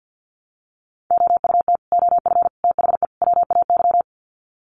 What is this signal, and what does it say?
Morse code, heard as a keyed tone, about 55 wpm, 700 Hz. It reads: OUT OF THE WAY
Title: Sets of 4 Words 50wpm